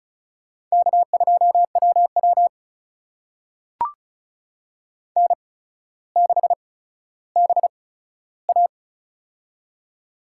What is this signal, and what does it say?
Morse code, heard as a keyed tone, about 36 wpm, spaced as for 15 wpm, 700 Hz. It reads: K2WW N6BA
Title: USA Call Signs - 4 Characters - 8x Character Spacing - 35wpm